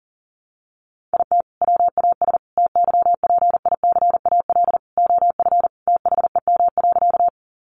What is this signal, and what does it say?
Morse code, heard as a keyed tone, about 40 wpm, 700 Hz. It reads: IT WAS TYPICAL OF THEM.
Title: Sentences from Top 1100 Words 40wpm